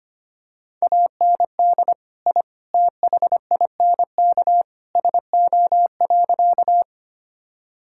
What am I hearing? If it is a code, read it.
Morse code, heard as a keyed tone, about 25 wpm, 700 Hz. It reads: AND I THINK SO.
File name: practice / 25wpm / Sentences with 3-Word Phrases - 25wpm